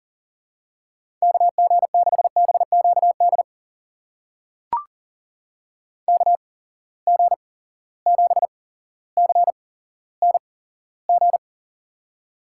Morse code, heard as a keyed tone, about 40 wpm, spaced as for 17 wpm, 700 Hz. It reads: KG6BQD KG7CNG
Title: USA Call Signs - 6 Characters - 8x Character Spacing - 40wpm